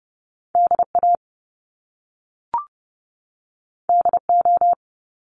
Morse code, heard as a keyed tone, about 30 wpm, 700 Hz. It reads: DA DO
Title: Top Two Letter Combinations in Words 30wpm